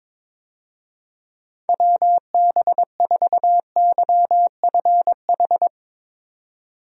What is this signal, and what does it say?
Morse code, heard as a keyed tone, about 22 wpm, 700 Hz. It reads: WB4YFH
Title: USA Call Signs - 6 Characters - 22wpm